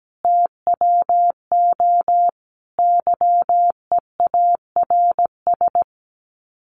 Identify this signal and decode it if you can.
Morse code, heard as a keyed tone, about 17 wpm, 700 Hz. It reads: TWO YEARS
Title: Sets of 5 Words 17wpm